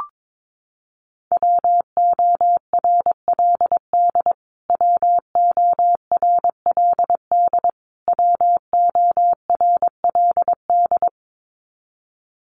Morse code, heard as a keyed tone, about 22 wpm, 700 Hz. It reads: WORLD WORLD WORLD
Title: Top 500 Words Repeated 22wpm